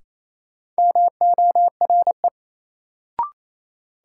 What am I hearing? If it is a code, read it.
Morse code, heard as a keyed tone, about 28 wpm, 700 Hz. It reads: MORE
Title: Sets of 2 Words - New Word Drill - 28wpm